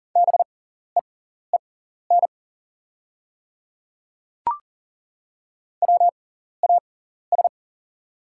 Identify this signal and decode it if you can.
Morse code, heard as a keyed tone, about 40 wpm, spaced as for 14 wpm, 700 Hz. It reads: BEEN WAS
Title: 6x Character Spacing - Top 200 Words - 40wpm